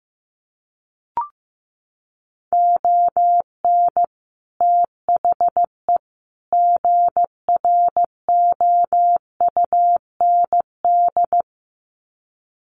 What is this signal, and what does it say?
Morse code, heard as a keyed tone, about 15 wpm, 700 Hz. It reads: ON THE GROUND
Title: Sets of 3 Words 15wpm